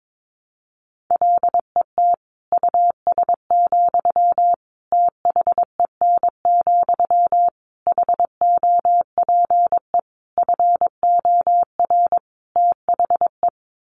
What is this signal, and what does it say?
Morse code, heard as a keyed tone, about 22 wpm, 700 Hz. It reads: LET US, THEN, HOPE FOR THE
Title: Sentences from Top 1000 Words 22wpm